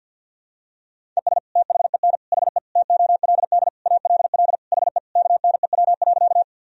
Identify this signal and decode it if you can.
Morse code, heard as a keyed tone, about 46 wpm, 700 Hz. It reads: ES THEN HE TOLD ALL HE KNEW.
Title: Sentences from Top 400 Words and Abbreviations 50wpm